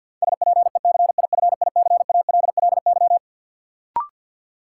Morse code, heard as a keyed tone, about 53 wpm, 700 Hz. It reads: SPECIFICALLY
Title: Sentences from Top 1100 Words - New Word Drill - 50wpm